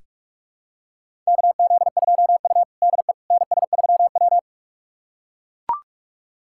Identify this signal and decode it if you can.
Morse code, heard as a keyed tone, about 42 wpm, 700 Hz. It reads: KZ2U DE NS3W
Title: USA Callsign DE Callsign - 4 Characters - 45wpm